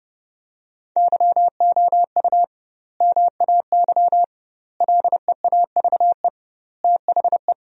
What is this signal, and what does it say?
Morse code, heard as a keyed tone, about 30 wpm, 700 Hz. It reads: YOU MAY LEAVE THE
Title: 2x Word-Spacing - Sentences from Top 400 Words - 30wpm